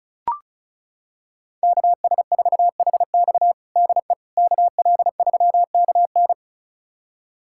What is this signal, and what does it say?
Morse code, heard as a keyed tone, about 36 wpm, 700 Hz. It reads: KS4HX DE KL3KN
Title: USA Callsign DE Callsign - 5 Characters - 35wpm